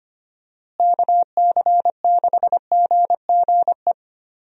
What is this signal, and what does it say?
Morse code, heard as a keyed tone, about 25 wpm, 700 Hz. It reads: KC6GGE
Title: USA Call Signs - 6 Characters - Repeated 3x - 25wpm